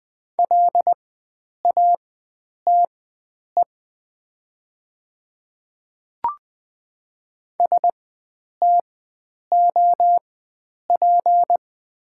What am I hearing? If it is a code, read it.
Morse code, heard as a keyed tone, about 20 wpm, spaced as for 9 wpm, 700 Hz. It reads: LATE STOP
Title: 4x Character Spacing - Top 300 Words - 20wpm